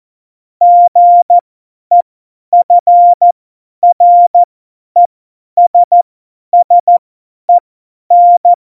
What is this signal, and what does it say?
Morse code, heard as a keyed tone, about 14 wpm, spaced as for 10 wpm, 700 Hz. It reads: GEFRESSEN
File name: Morsekurs / Witze / Witz 26. Juni 2018, 50 F70